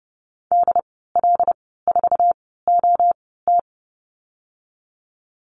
Morse code, heard as a keyed tone, about 30 wpm, spaced as for 17 wpm, 700 Hz. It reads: DL4OT
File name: Lesson 14g - Number 2 - Call Signs 30wpm